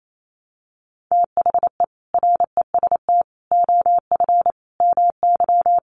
Morse code, heard as a keyed tone, about 28 wpm, 700 Hz. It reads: THE REST OF MY
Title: Sets of 5 Words 28wpm